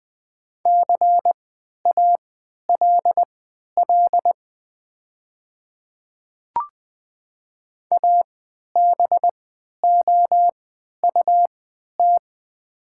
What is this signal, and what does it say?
Morse code, heard as a keyed tone, about 20 wpm, spaced as for 11 wpm, 700 Hz. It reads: CALL ABOUT
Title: 3x Character Spacing - Top 500 Words - 20wpm